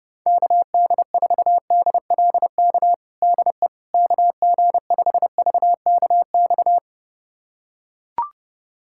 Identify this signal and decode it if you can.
Morse code, heard as a keyed tone, about 32 wpm, 700 Hz. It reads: KD4DLK DE KG5VKX
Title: USA Callsign DE Callsign - 6 Characters - 30wpm